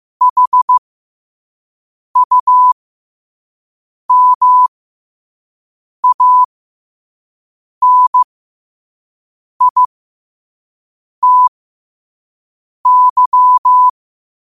Morse code, heard as a keyed tone, about 15 wpm, spaced as for 5 wpm, 1000 Hz. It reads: HUMANITY
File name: qotdfiles / 5wpm